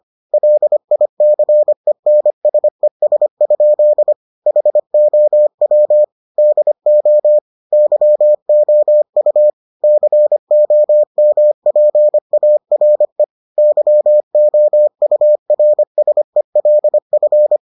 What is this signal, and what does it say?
Morse code, heard as a keyed tone, about 25 wpm, 600 Hz. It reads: LICENSES? HOW DO YOU COMPARE YOURSELF